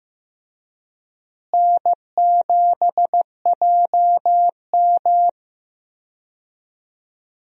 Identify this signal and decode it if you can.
Morse code, heard as a keyed tone, about 15 wpm, 700 Hz. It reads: N7JM
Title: USA Call Signs - 4 Characters - Repeated 3x - 15wpm